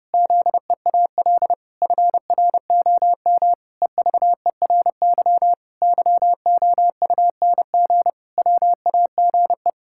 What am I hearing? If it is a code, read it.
Morse code, heard as a keyed tone, about 30 wpm, 700 Hz. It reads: ZEAL FROM EVERY YOUNG WAGE
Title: Pangram Challenge v2 - 30wpm